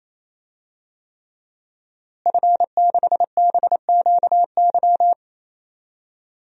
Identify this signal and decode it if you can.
Morse code, heard as a keyed tone, about 28 wpm, 700 Hz. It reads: F6BQY
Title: DXCC-100 Call Signs - 5 Characters - Repeated 3x - 28wpm